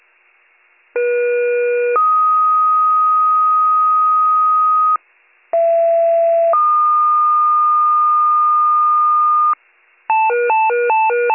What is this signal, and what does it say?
Two-tone page, heard as a one-second tone, then a three-second tone, about 480 then 1250 Hz, about 670 then 1220 Hz. It is heard from 1.0 to 9.5 s.